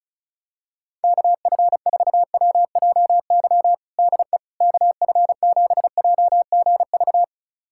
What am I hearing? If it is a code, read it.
Morse code, heard as a keyed tone, about 36 wpm, 700 Hz. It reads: KF4WJY DE KF7JGV
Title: USA Callsign DE Callsign 35wpm